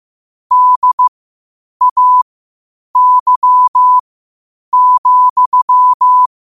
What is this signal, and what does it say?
Morse code, heard as a keyed tone, about 15 wpm, spaced as for 8 wpm, 1000 Hz. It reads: DAY,
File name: qotdfiles / 8wpm